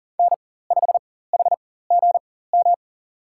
Morse code, heard as a keyed tone, about 37 wpm, spaced as for 18 wpm, 700 Hz. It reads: N5HGM
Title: USA Call Signs - 5 Characters - 4x Character Spacing - 40wpm